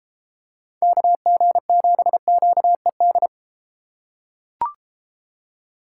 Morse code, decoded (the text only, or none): KG7QED